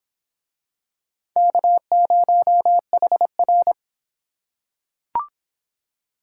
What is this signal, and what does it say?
Morse code, heard as a keyed tone, about 26 wpm, 700 Hz. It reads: K0HR